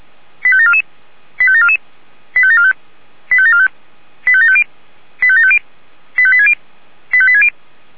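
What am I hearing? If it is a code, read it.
Five-tone selective call (ZVEI), heard as quick runs of five tones, tones of about 70 ms. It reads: 85744 85744 85745 85745 85750 85750 85860 85860
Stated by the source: Five-tone ZVEI selective call received on SDR.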